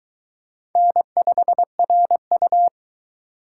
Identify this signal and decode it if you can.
Morse code, heard as a keyed tone, about 22 wpm, 700 Hz. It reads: N5RU